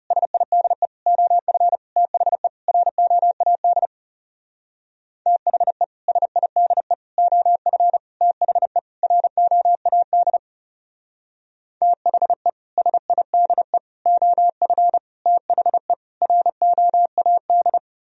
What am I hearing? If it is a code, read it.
Morse code, heard as a keyed tone, about 37 wpm, 700 Hz. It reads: SIDE OF THE ROAD THE SIDE OF THE ROAD THE SIDE OF THE ROAD
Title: Speed-Racing - Sets of 5 Words 30wpm